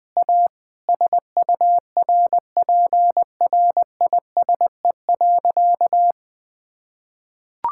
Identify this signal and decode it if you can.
Morse code, heard as a keyed tone, about 20 wpm, 700 Hz. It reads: A SURPRISE.
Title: Sentences with 2-Word Phrases - 20wpm